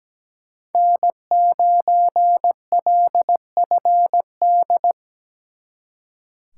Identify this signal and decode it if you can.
Morse code, heard as a keyed tone, about 20 wpm, 700 Hz. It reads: N9LFD
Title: Speed Racing - USA 1x3 Call Signs 17wpm